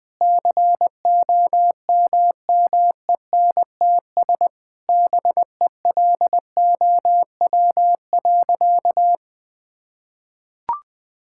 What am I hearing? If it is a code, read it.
Morse code, heard as a keyed tone, about 20 wpm, 700 Hz. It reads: COMMENTS BELOW.
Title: Sentences with 9-Word Phrases - 20wpm